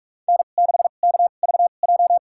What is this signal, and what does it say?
Morse code, heard as a keyed tone, about 46 wpm, spaced as for 32 wpm, 700 Hz. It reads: N6KVJ
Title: USA Call Signs - 2x Character Spacing - 45wpm